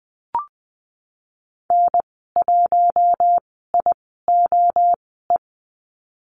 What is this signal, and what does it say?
Morse code, heard as a keyed tone, about 20 wpm, spaced as for 14 wpm, 700 Hz. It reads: N1IOE
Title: 042 Lesson 8g - Number 1 - Call Signs 20wpm